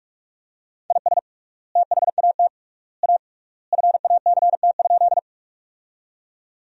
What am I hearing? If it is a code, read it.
Morse code, heard as a keyed tone, about 42 wpm, 700 Hz. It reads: IS THAT A FACT?
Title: 3x Word-Spacing - Sentences from Top 500 Words - 45wpm